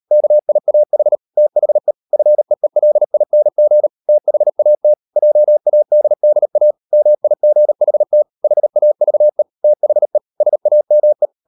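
Morse code, heard as a keyed tone, about 38 wpm, 600 Hz. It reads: KIAH THE FEELING THAT JADDA MIGHT HAVE THE SAME
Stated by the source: Morse code practice at 38 WPM, sending Ken Follett's novel Never in English, from CW med Gnister.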